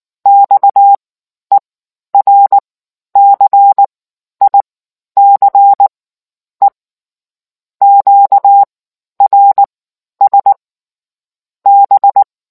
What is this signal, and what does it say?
Morse code, heard as a keyed tone, about 19 wpm, spaced as for 11 wpm, 800 Hz. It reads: XERCICE QRS B